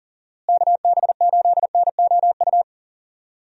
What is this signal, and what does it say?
Morse code, heard as a keyed tone, about 39 wpm, 700 Hz. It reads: KB8NOU
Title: USA Call Signs - 6 Characters - Repeated 2x - 40wpm